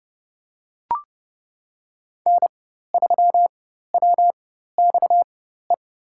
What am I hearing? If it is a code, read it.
Morse code, heard as a keyed tone, about 30 wpm, spaced as for 14 wpm, 700 Hz. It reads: N3WXE
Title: USA Call Signs - 5 Characters - 4x Character Spacing - 30wpm